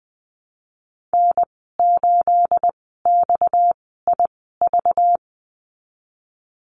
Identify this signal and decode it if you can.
Morse code, heard as a keyed tone, about 20 wpm, spaced as for 14 wpm, 700 Hz. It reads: N8XIV
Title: 231 Lesson 37g - Letter x - Call Signs 20wpm